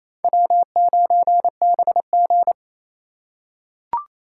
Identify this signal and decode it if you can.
Morse code, heard as a keyed tone, about 27 wpm, 700 Hz. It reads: W9BG
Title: USA Call Signs - 4 Characters - 4x Character Spacing - 28wpm